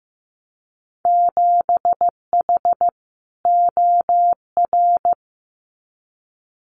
Morse code, heard as a keyed tone, about 15 wpm, 700 Hz. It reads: 7H OR
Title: Winter Field Day Exchanges Encore 15wpm